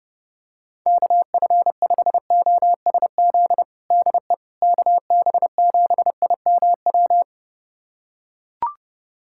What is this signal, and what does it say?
Morse code, heard as a keyed tone, about 30 wpm, 700 Hz. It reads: KF5OSZ DE KB7IMW
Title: USA Callsign DE Callsign 30wpm